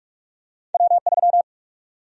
Morse code, heard as a keyed tone, about 46 wpm, 700 Hz. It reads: W3
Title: USA Call Sign Prefixes - 45wpm